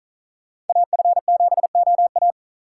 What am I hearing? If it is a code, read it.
Morse code, heard as a keyed tone, about 42 wpm, 700 Hz. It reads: AF7OA
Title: Speed Racing - USA Call Signs 35wpm